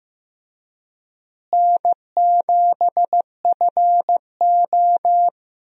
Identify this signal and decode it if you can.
Morse code, heard as a keyed tone, about 15 wpm, 700 Hz. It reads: N7FO
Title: USA Call Signs - 4 Characters - Repeated 3x - 15wpm